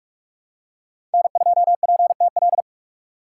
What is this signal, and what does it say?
Morse code, heard as a keyed tone, about 46 wpm, 700 Hz. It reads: N2PTL
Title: USA Call Signs - 4x Character Spacing - 45wpm